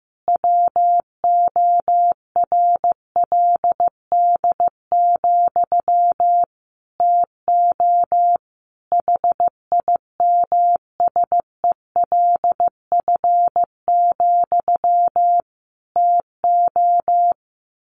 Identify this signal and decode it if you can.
Morse code, heard as a keyed tone, about 15 wpm, 700 Hz. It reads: WORLD, TO HIMSELF, TO
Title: A Sound Of Thunder 15wpm